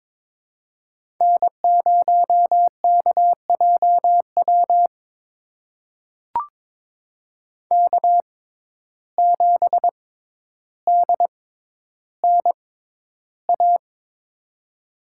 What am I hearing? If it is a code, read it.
Morse code, heard as a keyed tone, about 22 wpm, spaced as for 11 wpm, 700 Hz. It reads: N0KJW K7DNA
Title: USA Call Signs - 5 Characters - 6x Character Spacing - 22wpm